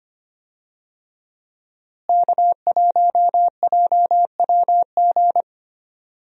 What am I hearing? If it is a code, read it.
Morse code, heard as a keyed tone, about 25 wpm, 700 Hz. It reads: K1JWG
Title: USA Call Signs - 5 Characters - Repeated 3x - 25wpm